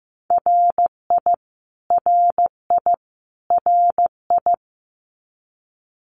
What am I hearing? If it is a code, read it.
Morse code, heard as a keyed tone, about 15 wpm, 700 Hz. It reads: RI RI RI
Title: US States Repeated 3x 15wpm